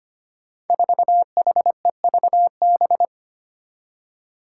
Morse code, heard as a keyed tone, about 25 wpm, 700 Hz. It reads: 4HEVB
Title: Five Challenging Letter-Number Combinations 25wpm